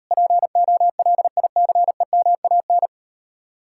Morse code, heard as a keyed tone, about 38 wpm, 700 Hz. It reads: POLICEMAN